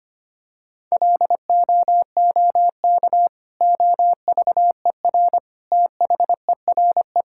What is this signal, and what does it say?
Morse code, heard as a keyed tone, about 25 wpm, 700 Hz. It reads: LOOK OVER THERE
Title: Sentences from Top 1600 Words 25wpm